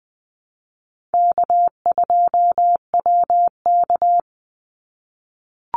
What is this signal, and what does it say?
Morse code, heard as a keyed tone, about 20 wpm, 700 Hz. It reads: K2WK
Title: Call Signs - 20wpm